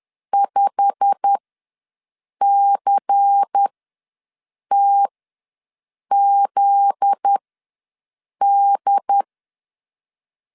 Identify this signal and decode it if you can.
Morse code, heard as a keyed tone, about 10 wpm, spaced as for 6 wpm, 800 Hz. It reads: 5CTZD
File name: LEZIONI / 9 286 ws 10wpm 4032